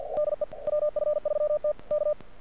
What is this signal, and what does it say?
Morse code, heard as a keyed tone, about 45 wpm, 600 Hz. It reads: DE WF3T K
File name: archived / CRSnet